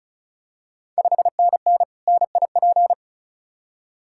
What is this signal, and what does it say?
Morse code, heard as a keyed tone, about 36 wpm, 700 Hz. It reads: 5NN NIP
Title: Ontario QSO Party Exchange 35wpm